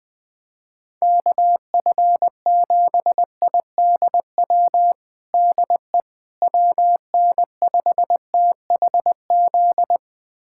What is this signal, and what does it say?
Morse code, heard as a keyed tone, about 21 wpm, 700 Hz. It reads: KF7IDW DE WN5THZ
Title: USA Callsign DE Callsign - 6 Characters - 20wpm